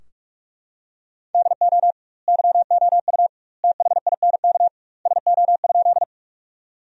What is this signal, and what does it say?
Morse code, heard as a keyed tone, about 42 wpm, 700 Hz. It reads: DO YOU THINK SO?
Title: 2x Word-Spacing - Sentences from Top 200 Words - 45wpm